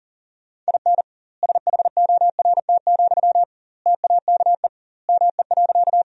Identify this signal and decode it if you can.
Morse code, heard as a keyed tone, about 40 wpm, 700 Hz. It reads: IN SHORT, TAKE ME.
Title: 2x Word-Spacing - Sentences from Top 400 Words - 40wpm